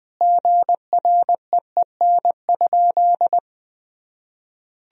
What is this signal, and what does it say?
Morse code, heard as a keyed tone, about 21 wpm, 700 Hz. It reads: GREEN?
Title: Sentences from Top 1000 Words - Review Part 2 - 20wpm